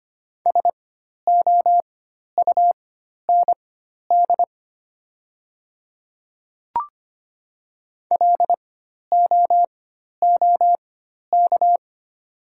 Morse code, heard as a keyed tone, about 25 wpm, spaced as for 12 wpm, 700 Hz. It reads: SOUND LOOK
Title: 4x Character Spacing - Top 100 Words - 25wpm